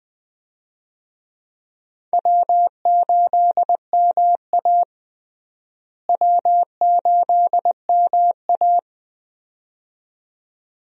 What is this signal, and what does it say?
Morse code, heard as a keyed tone, about 20 wpm, 700 Hz. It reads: W8MA W8MA
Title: USA Call Signs - 4 Characters - Repeated 2x - 20wpm